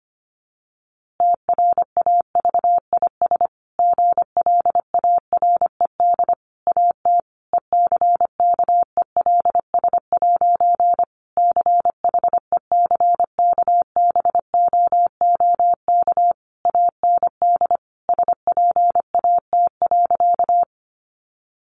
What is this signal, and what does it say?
Morse code, heard as a keyed tone, about 25 wpm, 700 Hz. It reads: TRAVIS GLARED AT ECKELS' CHECKBOOK AND SPAT.
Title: A Sound Of Thunder 25wpm